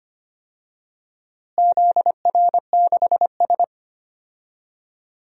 Morse code, heard as a keyed tone, about 25 wpm, 700 Hz. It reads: ZR6S
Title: DXCC-100 Call Signs - 4 Characters - Repeated 3x - 25wpm